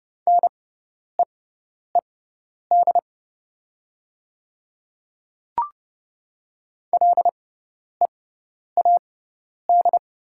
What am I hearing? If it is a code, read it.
Morse code, heard as a keyed tone, about 30 wpm, spaced as for 10 wpm, 690 Hz. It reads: NEED LEAD
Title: 6x Character Spacing - Top 500 Words - 30wpm